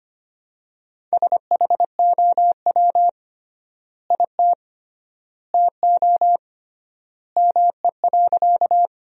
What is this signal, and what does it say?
Morse code, heard as a keyed tone, about 25 wpm, 700 Hz. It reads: SHOW IT TO ME.
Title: 3x Word-Spacing - Sentences from Top 200 Words - 25wpm